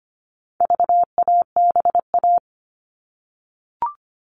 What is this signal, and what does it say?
Morse code, heard as a keyed tone, about 25 wpm, 700 Hz. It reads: VABA
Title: All Cut Numbers - 4-Digit 25wpm